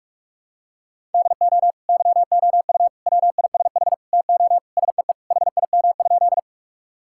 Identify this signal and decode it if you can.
Morse code, heard as a keyed tone, about 42 wpm, 700 Hz. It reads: DO YOU WISH TO SEE HIM?
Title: Sentences from Top 1600 Words 45wpm